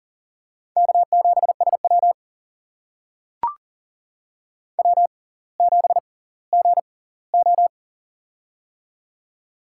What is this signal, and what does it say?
Morse code, heard as a keyed tone, about 40 wpm, spaced as for 20 wpm, 700 Hz. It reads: K7SW W7GO
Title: USA Call Signs - 6x Character Spacing - 40wpm